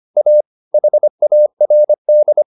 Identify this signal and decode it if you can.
Morse code, heard as a keyed tone, about 25 wpm, 600 Hz. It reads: A HARD